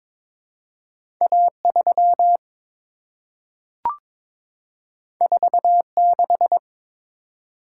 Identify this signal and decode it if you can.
Morse code, heard as a keyed tone, about 22 wpm, 700 Hz. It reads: A3 46
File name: Two Letter-Number Combinations - 22wpm